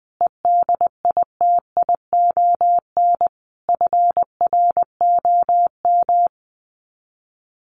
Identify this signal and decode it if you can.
Morse code, heard as a keyed tone, about 20 wpm, 700 Hz. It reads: EDITION FROM
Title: Sets of 5 Words Encore2 20wpm